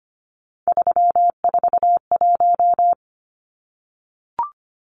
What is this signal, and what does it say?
Morse code, heard as a keyed tone, about 25 wpm, 700 Hz. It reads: 341
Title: Numbers - 3-Digit 25wpm